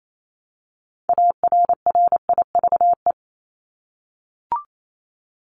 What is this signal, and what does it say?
Morse code, heard as a keyed tone, about 28 wpm, 700 Hz. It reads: ARRIVE
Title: ED Suffix - Top 100 Words with Base Words 28wpm